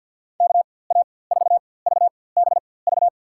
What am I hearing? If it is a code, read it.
Morse code, heard as a keyed tone, about 53 wpm, spaced as for 24 wpm, 700 Hz. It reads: KA4VBV
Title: USA Call Signs - 6 Characters - 4x Character Spacing - 50wpm